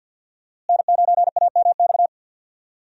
Morse code, heard as a keyed tone, about 48 wpm, 700 Hz. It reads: N9AMX
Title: USA Call Signs - 5 Characters - Repeated 3x - 50wpm